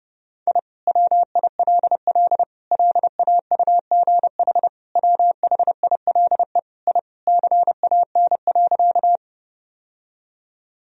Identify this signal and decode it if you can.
Morse code, heard as a keyed tone, about 32 wpm, 700 Hz. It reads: I WILL LAUGH WHILE I CAN.
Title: Sentences from Top 1000 Words - Review Part 1 - 30wpm